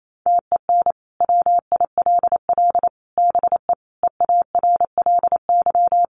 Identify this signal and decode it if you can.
Morse code, heard as a keyed tone, about 28 wpm, 700 Hz. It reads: TEN WILL BE EARLY
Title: Sentences from Top 400 Words 28wpm